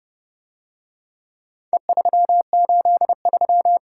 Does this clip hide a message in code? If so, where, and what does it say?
Yes: Morse code from 1.7 to 3.8 s, heard as a keyed tone, about 28 wpm, 700 Hz. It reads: E383